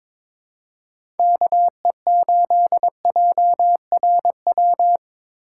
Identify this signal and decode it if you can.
Morse code, heard as a keyed tone, about 22 wpm, 700 Hz. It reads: KE8JRW
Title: USA Call Signs - 2x Character Spacing - 22wpm